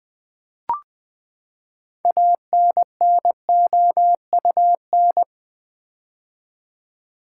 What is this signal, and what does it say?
Morse code, heard as a keyed tone, about 20 wpm, 700 Hz. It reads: ANNOUN
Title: Build A Word - 6 Letters At A Time - 12-Letter Words - 20wpm